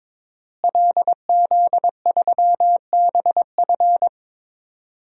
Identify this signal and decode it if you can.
Morse code, heard as a keyed tone, about 22 wpm, 700 Hz. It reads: LZ3BF
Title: DXCC-100 Call Signs - 5 Characters - Repeated 3x - 22wpm